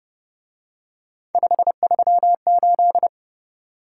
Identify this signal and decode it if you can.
Morse code, heard as a keyed tone, about 27 wpm, 700 Hz. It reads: 538
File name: Speed-Racing - Three Letter-Number Combinations - 22wpm